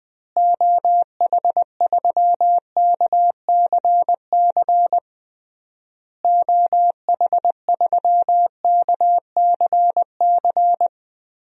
Morse code, heard as a keyed tone, about 20 wpm, 700 Hz. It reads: OH3KCC OH3KCC
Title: DXCC-100 Call Signs - 6 Characters - Repeated 3x - 20wpm